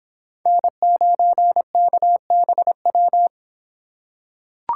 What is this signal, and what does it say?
Morse code, heard as a keyed tone, about 26 wpm, 700 Hz. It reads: N9KBW